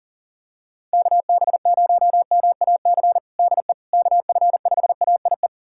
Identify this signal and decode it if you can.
Morse code, heard as a keyed tone, about 40 wpm, 700 Hz. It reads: KB0MAC DE KF5AIE
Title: USA Callsign DE Callsign - 6 Characters - 40wpm